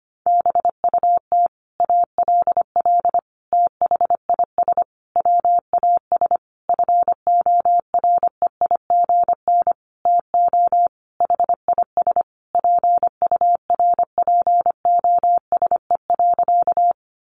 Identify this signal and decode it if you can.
Morse code, heard as a keyed tone, about 25 wpm, 700 Hz. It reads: BUT ALL THIS WAS FOREIGN TO HIS PURPOSE.
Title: Sentences from Top 1400 Words 25wpm